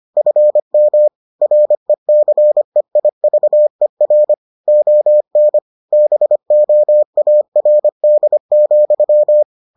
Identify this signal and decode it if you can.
Morse code, heard as a keyed tone, about 25 wpm, 600 Hz. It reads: FM RECEIVER ON BOARD,